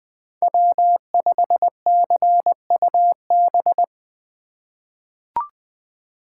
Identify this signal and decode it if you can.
Morse code, heard as a keyed tone, about 20 wpm, 700 Hz. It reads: W5CUB
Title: USA Call Signs - 5 Characters - 3x Character Spacing - 20wpm